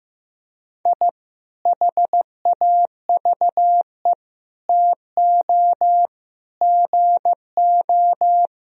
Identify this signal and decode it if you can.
Morse code, heard as a keyed tone, about 15 wpm, 700 Hz. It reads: I HAVE TO GO
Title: Sentences from Top 1000 Words - Review Part 1 - 15wpm